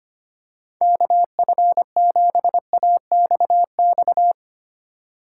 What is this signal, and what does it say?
Morse code, heard as a keyed tone, about 25 wpm, 700 Hz. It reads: KF7AXX
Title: USA Call Signs - 6 Characters - 3x Character Spacing - 25wpm